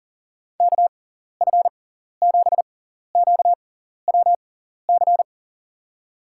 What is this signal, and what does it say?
Morse code, heard as a keyed tone, about 40 wpm, spaced as for 14 wpm, 700 Hz. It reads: KF7QWC
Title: USA Call Signs - 6x Character Spacing - 40wpm